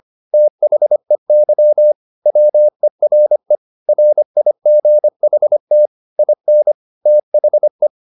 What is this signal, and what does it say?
Morse code, heard as a keyed tone, about 25 wpm, 600 Hz. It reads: THEY WERE RIGHT IN THE
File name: the old man and the sea part 1 25wpm